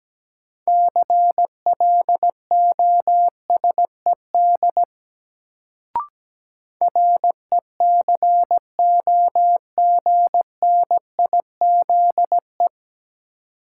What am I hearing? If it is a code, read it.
Morse code, heard as a keyed tone, about 17 wpm, 700 Hz. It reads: CLOSED RECOGNIZE
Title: Sentences from Top 1400 Words - New Word Drill - 17wpm